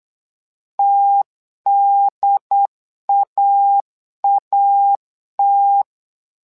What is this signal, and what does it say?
Morse code, heard as a keyed tone, about 8 wpm, 800 Hz. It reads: TDAAT
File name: LEZIONI / 2 UHDN / 2 UHDN 8wpm2232